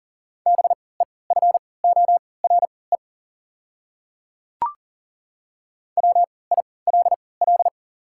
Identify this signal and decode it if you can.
Morse code, heard as a keyed tone, about 40 wpm, spaced as for 23 wpm, 690 Hz. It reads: BEFORE WILL
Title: 3x Character Spacing - Top 200 Words - 40wpm